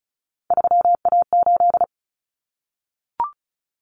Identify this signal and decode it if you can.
Morse code, heard as a keyed tone, about 34 wpm, 700 Hz. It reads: 3A8
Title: Common Cut Numbers - 3-Digit 35wpm